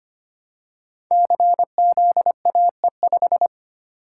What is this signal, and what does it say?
Morse code, heard as a keyed tone, about 25 wpm, 700 Hz. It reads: CZAE5
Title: Code Groups - 5 Letter:Numbers - 25wpm